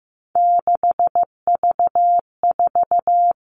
Morse code, heard as a keyed tone, about 15 wpm, 700 Hz. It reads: 6V4
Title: Three Challenging Letter-Number Combinations 15wpm